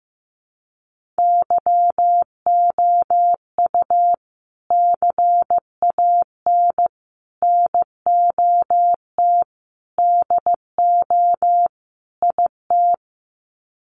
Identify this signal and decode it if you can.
Morse code, heard as a keyed tone, about 15 wpm, 700 Hz. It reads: YOU CAN NOT DO IT
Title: Sets of 5 Words Encore2 15wpm